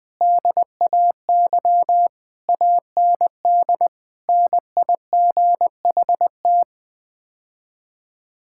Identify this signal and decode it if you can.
Morse code, heard as a keyed tone, about 20 wpm, 700 Hz. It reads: DAY AND NIGHT
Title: Sets of 8 Words - 20wpm